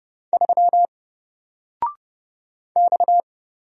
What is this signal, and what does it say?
Morse code, heard as a keyed tone, about 31 wpm, 700 Hz. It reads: 3 X
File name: Single-Letter-Number-30wpm